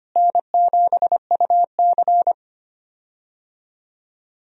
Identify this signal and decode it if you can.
Morse code, heard as a keyed tone, about 25 wpm, 700 Hz. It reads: N7UC
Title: USA Call Signs - 4 Characters - Repeated 2x - 25wpm